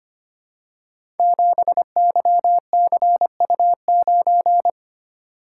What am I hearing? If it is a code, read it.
Morse code, heard as a keyed tone, about 25 wpm, 700 Hz. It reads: 7YCU9
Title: Code Groups - 5 Letter:Numbers - 25wpm